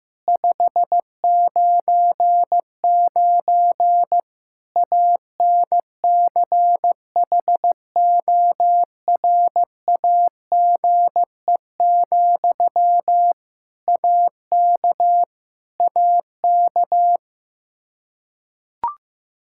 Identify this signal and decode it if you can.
Morse code, heard as a keyed tone, about 16 wpm, 700 Hz. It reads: 599 ANCHORAGE, AK AK
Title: Practice QSO - Exchange 1 - Beginning - 15wpm